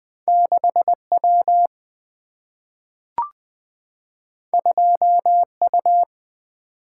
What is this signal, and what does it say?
Morse code, heard as a keyed tone, about 20 wpm, 700 Hz. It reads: 6W 2U
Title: Single-Letter Single-Number 20wpm